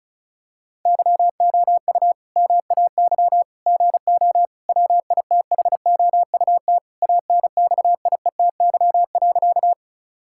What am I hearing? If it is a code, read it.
Morse code, heard as a keyed tone, about 34 wpm, 700 Hz. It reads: YOU MAY GO WITHOUT ANXIETY.
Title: Sentences from Top 1600 Words 35wpm